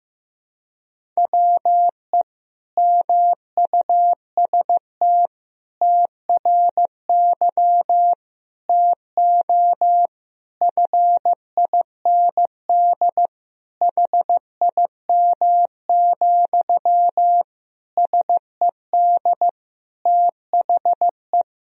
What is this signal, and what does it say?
Morse code, heard as a keyed tone, about 15 wpm, 700 Hz. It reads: WE MUST TRY TO FIND HIM, SED THE
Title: Sentences from Top 400 Words and Abbreviations 15wpm